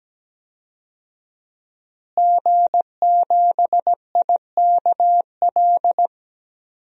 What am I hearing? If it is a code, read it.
Morse code, heard as a keyed tone, about 17 wpm, 700 Hz. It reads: G7IKL
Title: DXCC-100 Call Signs - 5 Characters - Repeated 2x - 17wpm